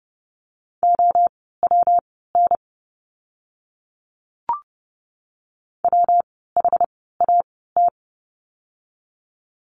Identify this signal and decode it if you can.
Morse code, heard as a keyed tone, about 30 wpm, spaced as for 17 wpm, 700 Hz. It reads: OWN WHAT
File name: course / 30:10wpm / 237 Lesson 38f - Letter q - Words 30wpm